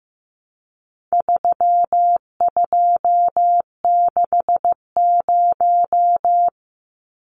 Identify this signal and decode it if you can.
Morse code, heard as a keyed tone, about 15 wpm, 700 Hz. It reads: 3260
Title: Numbers - 4-Digit 15wpm